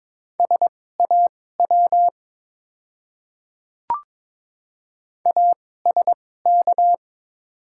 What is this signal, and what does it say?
Morse code, heard as a keyed tone, about 22 wpm, spaced as for 16 wpm, 690 Hz. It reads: SAW ASK
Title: 2x Character Spacing - Top 500 Words - 22wpm